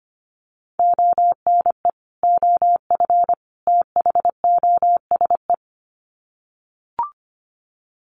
Morse code, heard as a keyed tone, about 25 wpm, 700 Hz. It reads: ONE OF THOSE
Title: Sets of 3 Words 25wpm